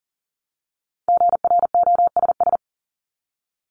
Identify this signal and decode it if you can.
Morse code, heard as a keyed tone, about 40 wpm, 700 Hz. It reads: GROSS
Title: Top 1751-2000 Words 40wpm